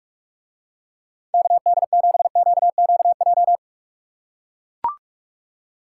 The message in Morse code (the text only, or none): KD7QQJ